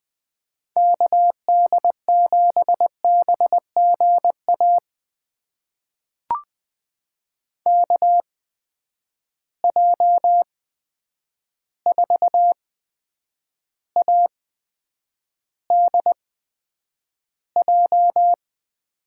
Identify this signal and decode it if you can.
Morse code, heard as a keyed tone, about 20 wpm, spaced as for 9 wpm, 700 Hz. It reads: KD7BGA KJ4ADJ
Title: USA Call Signs - 6 Characters - 8x Character Spacing - 20wpm